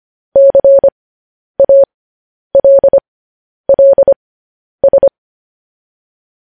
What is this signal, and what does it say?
Morse code, heard as a keyed tone, about 25 wpm, spaced as for 10 wpm, 550 Hz. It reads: CALLS